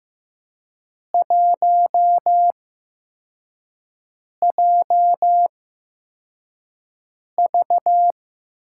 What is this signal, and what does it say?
Morse code, heard as a keyed tone, about 15 wpm, spaced as for 4 wpm, 700 Hz. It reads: 1JV
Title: USA Call Signs - 4 Characters - 8x Character Spacing - 15wpm